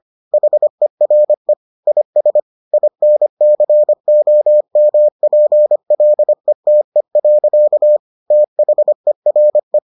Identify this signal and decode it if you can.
Morse code, heard as a keyed tone, about 25 wpm, 600 Hz. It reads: HERE IS INCOMPLETE. THERE